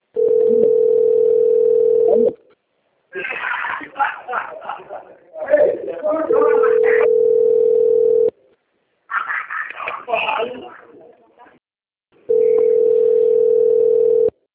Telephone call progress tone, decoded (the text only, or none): ringback tone